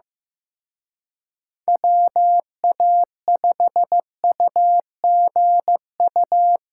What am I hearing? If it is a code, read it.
Morse code, heard as a keyed tone, about 18 wpm, 700 Hz. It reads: WA5UGU
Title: Speed Racing - USA 2x3 Call Signs 15wpm